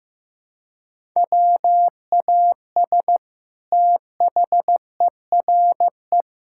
Morse code, heard as a keyed tone, about 16 wpm, 700 Hz. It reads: WAS THERE
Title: Sentences from Top 1000 Words - Review Part 6 - 15wpm